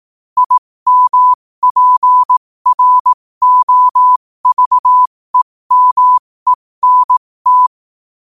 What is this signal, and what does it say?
Morse code, heard as a keyed tone, about 18 wpm, spaced as for 15 wpm, 1000 Hz. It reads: IMPROVEMENT
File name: qotdfiles / 15wpm